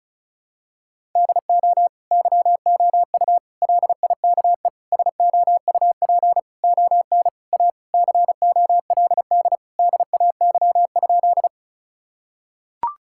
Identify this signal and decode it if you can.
Morse code, heard as a keyed tone, about 36 wpm, 700 Hz. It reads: DO YOU LIKE SOUP ON A COLD DAY?
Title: Sentences from Top 1000 Words - Review Part 6 - 35wpm